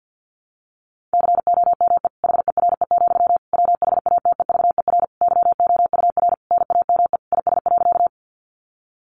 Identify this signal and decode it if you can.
Morse code, heard as a keyed tone, about 50 wpm, 700 Hz. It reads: COME HERE, WHATEVER YOUR NAME IS.
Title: Sentences from Top 1100 Words 50wpm